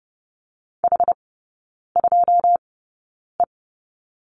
Morse code, heard as a keyed tone, about 30 wpm, spaced as for 9 wpm, 700 Hz. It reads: H2E
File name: 081 Lesson 14e - Number 2 - Three Characters 30wpm